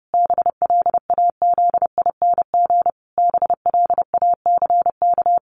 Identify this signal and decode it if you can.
Morse code, heard as a keyed tone, about 28 wpm, 700 Hz. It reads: BLAZING BLACK
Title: A Sound Of Thunder 30wpm